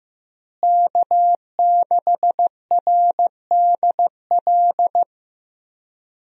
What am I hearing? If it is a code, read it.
Morse code, heard as a keyed tone, about 15 wpm, 700 Hz. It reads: K6RDL
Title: USA Call Signs - 5 Characters - Repeated 3x - 15wpm